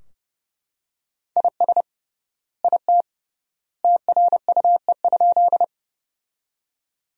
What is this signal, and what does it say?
Morse code, heard as a keyed tone, about 30 wpm, 700 Hz. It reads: IS IT TRUE?
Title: 3x Word-Spacing - Sentences from Top 400 Words - 30wpm